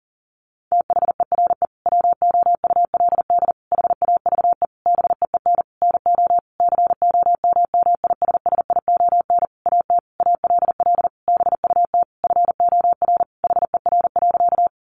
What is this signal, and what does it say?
Morse code, heard as a keyed tone, about 39 wpm, 700 Hz. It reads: THERE WOULD HAVE BEEN NO COMMISSION AT ALL BUT FOR HER.
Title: Sentences from Top 1400 Words 40wpm